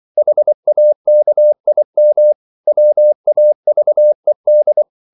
Morse code, heard as a keyed tone, about 24 wpm, 600 Hz. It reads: HAKIM WAVED